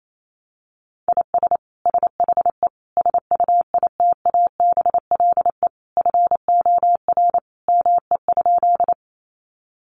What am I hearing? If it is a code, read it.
Morse code, heard as a keyed tone, about 28 wpm, 700 Hz. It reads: IS SHE SUITABLE FOR ME?
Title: Sentences from Top 1400 Words 28wpm